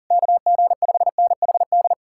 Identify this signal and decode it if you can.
Morse code, heard as a keyed tone, about 52 wpm, 700 Hz. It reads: KG5NHD